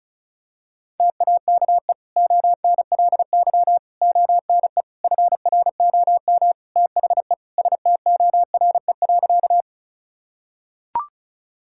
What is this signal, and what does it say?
Morse code, heard as a keyed tone, about 36 wpm, 700 Hz. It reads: TAKE ONLY ONE FROM THE STORE.
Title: Sentences from Top 1000 Words - Review Part 5 - 35wpm